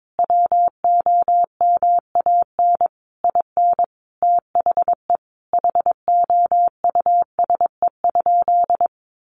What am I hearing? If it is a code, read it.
Morse code, heard as a keyed tone, about 22 wpm, 700 Hz. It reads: WOMAN IN THE HOUSE?
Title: Sentences from Top 1100 Words 22wpm